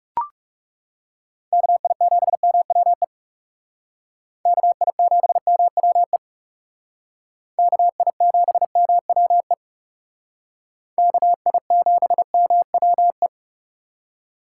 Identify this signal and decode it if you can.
Morse code, heard as a keyed tone, about 38 wpm, 700 Hz. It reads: KI7MWE KI7MWE KI7MWE KI7MWE